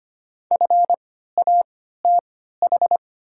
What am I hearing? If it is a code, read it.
Morse code, heard as a keyed tone, about 25 wpm, spaced as for 14 wpm, 700 Hz. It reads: FATH
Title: 3x Character Spacing - Top 200 Words - 25wpm